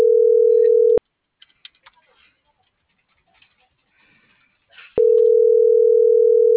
Ringback tone (North America), 440 + 480 Hz.